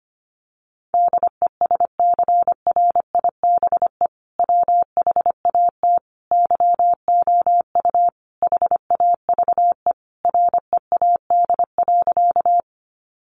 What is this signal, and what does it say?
Morse code, heard as a keyed tone, about 25 wpm, 700 Hz. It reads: DESCRIBE WHAT YOU HAVE READ.
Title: Sentences from Top 1000 Words 25wpm